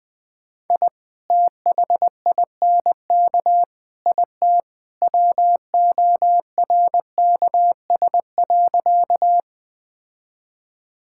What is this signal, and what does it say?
Morse code, heard as a keyed tone, about 21 wpm, 700 Hz. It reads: I THINK IT WORKS.
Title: Sentences with 2-Word Phrases - 20wpm